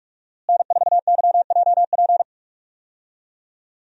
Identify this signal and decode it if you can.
Morse code, heard as a keyed tone, about 46 wpm, 700 Hz. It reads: N4YJP
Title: USA Call Signs - 5 Characters - Repeated 3x - 45wpm